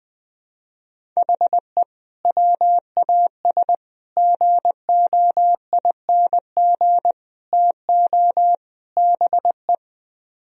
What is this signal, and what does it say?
Morse code, heard as a keyed tone, about 20 wpm, 700 Hz. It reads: HE WAS GOING TO BE
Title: Speed-Racing - Sets of 5 Words 17wpm